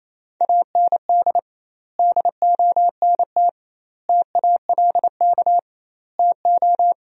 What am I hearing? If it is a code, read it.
Morse code, heard as a keyed tone, about 28 wpm, 700 Hz. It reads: AND DONT TALK TO
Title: 2x Word-Spacing - Sentences from Top 400 Words - 28wpm